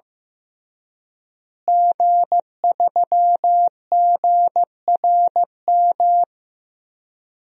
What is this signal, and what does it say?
Morse code, heard as a keyed tone, about 15 wpm, 700 Hz. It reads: G3GRM
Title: DXCC-100 Call Signs - 5 Characters - Repeated 3x - 15wpm